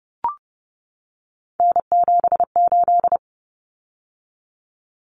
Morse code, heard as a keyed tone, about 30 wpm, 700 Hz. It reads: N78
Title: Common Cut Numbers - 3-Digit 30wpm